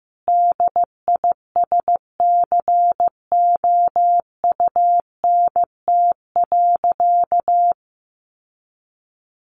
Morse code, heard as a keyed tone, about 15 wpm, 700 Hz. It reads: DISCOUNT.
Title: Sentences from Top 1200 Words 15wpm